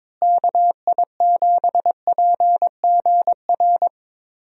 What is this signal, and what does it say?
Morse code, heard as a keyed tone, about 22 wpm, 700 Hz. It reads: KI7PGR
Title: USA Call Signs - 6 Characters - Repeated 3x - 22wpm